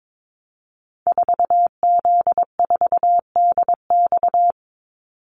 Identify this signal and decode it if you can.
Morse code, heard as a keyed tone, about 22 wpm, 690 Hz. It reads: 4Z4DX
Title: Call Signs Repeated 2x - 22wpm